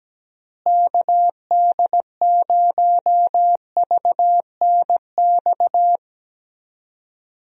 Morse code, heard as a keyed tone, about 17 wpm, 700 Hz. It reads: KD0VNX
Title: USA Call Signs - 6 Characters - Repeated 2x - 17wpm